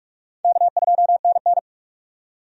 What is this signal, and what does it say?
Morse code, heard as a keyed tone, about 46 wpm, 700 Hz. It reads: K2NN
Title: USA Call Signs - 4 Characters - 45wpm